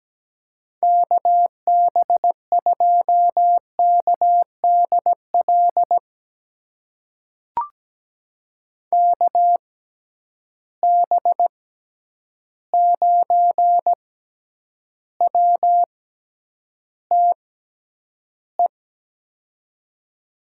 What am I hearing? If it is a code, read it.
Morse code, heard as a keyed tone, about 17 wpm, spaced as for 9 wpm, 700 Hz. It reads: KB2KDL KB9WTE
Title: USA Call Signs - 6 Characters - 6x Character Spacing - 17wpm